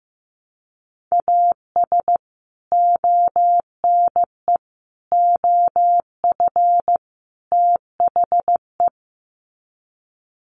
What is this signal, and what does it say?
Morse code, heard as a keyed tone, about 15 wpm, 700 Hz. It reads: AS ONE OF THE
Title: Sets of 4 Words 15wpm